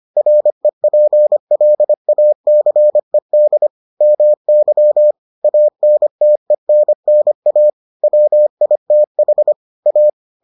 Morse code, heard as a keyed tone, about 25 wpm, 600 Hz. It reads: REPLACED MY ANTENNA WITH A